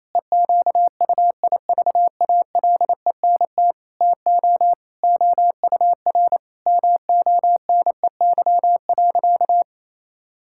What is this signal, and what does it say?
Morse code, heard as a keyed tone, about 28 wpm, 700 Hz. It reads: EQUIVALENT TO OUR MONEY.
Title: Sentences from Top 1600 Words 28wpm